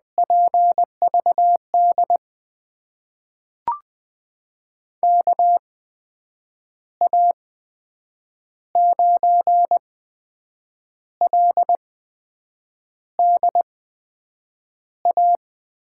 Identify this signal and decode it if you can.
Morse code, heard as a keyed tone, about 20 wpm, spaced as for 7 wpm, 700 Hz. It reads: PVD KA9LDA
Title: USA Call Signs - 6 Characters - 8x Character Spacing - 20wpm